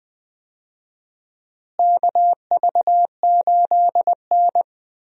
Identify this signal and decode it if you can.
Morse code, heard as a keyed tone, about 20 wpm, 700 Hz. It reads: KV8N
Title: USA Call Signs - 4 Characters - Repeated 3x - 20wpm